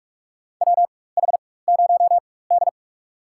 Morse code, heard as a keyed tone, about 46 wpm, spaced as for 21 wpm, 690 Hz. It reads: WH0D
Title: USA Call Signs - 4 Characters - 4x Character Spacing - 45wpm